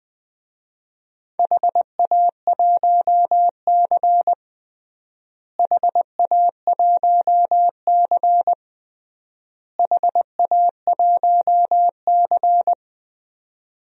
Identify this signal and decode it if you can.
Morse code, heard as a keyed tone, about 20 wpm, 700 Hz. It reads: HA1C HA1C HA1C
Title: DXCC-100 Call Signs - 4 Characters - Repeated 3x - 20wpm